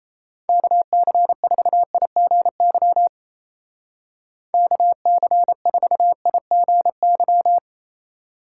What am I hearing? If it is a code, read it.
Morse code, heard as a keyed tone, about 33 wpm, 700 Hz. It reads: KC4IGY KC4IGY
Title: Speed Racing - USA Call Signs 28wpm